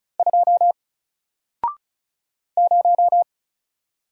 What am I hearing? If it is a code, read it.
Morse code, heard as a keyed tone, about 35 wpm, 700 Hz. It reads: 2 0
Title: Single Character (Letter:Number:Punctuation:Prosign) 35wpm